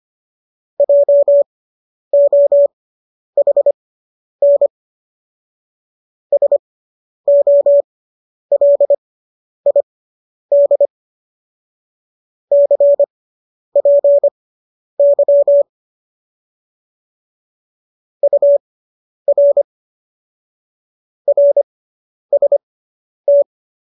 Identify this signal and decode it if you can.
Morse code, heard as a keyed tone, about 25 wpm, spaced as for 10 wpm, 570 Hz. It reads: JOHN SOLID CPY UR RST